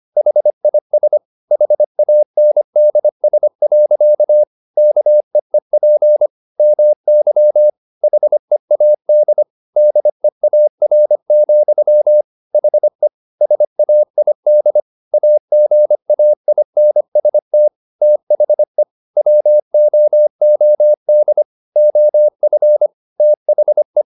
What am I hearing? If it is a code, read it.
Morse code, heard as a keyed tone, about 25 wpm, 600 Hz. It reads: HIS HANDS. KEEP MY HEAD DEAR, HE SAID AGAINST THE WOOD OF THE